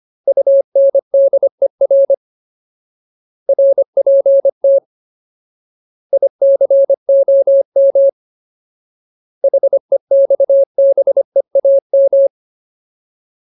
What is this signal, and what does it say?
Morse code, heard as a keyed tone, about 25 wpm, 550 Hz. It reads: UNDER RPT ICOM HEXBEAM
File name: Ultimate Word List / 25WPM Ultimate